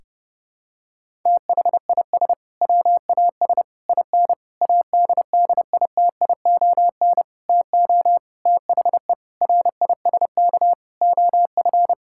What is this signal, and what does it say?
Morse code, heard as a keyed tone, about 30 wpm, 700 Hz. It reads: THIS WAS IN ADDITION TO THE RISK OF
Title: Sentences from Top 1000 Words - Review Part 6 - 30wpm